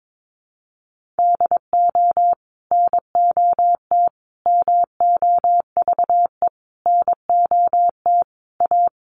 Morse code, heard as a keyed tone, about 22 wpm, 700 Hz. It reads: DO NOT MOVE NOT A
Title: Sentences from Top 1300 Words 22wpm